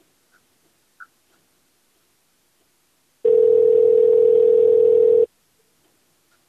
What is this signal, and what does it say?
Ringback tone (North America), 440 + 480 Hz.